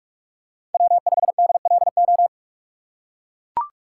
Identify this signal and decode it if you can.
Morse code, heard as a keyed tone, about 42 wpm, 700 Hz. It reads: W5DLO
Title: USA Call Signs - 5 Characters - 4x Character Spacing - 45wpm